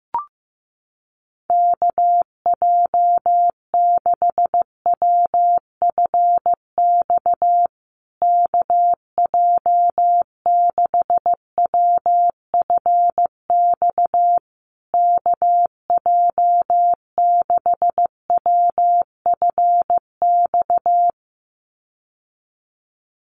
Morse code, heard as a keyed tone, about 15 wpm, 700 Hz. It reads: KJ6WFX KJ6WFX KJ6WFX
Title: USA Call Signs Repeated 3x 15wpm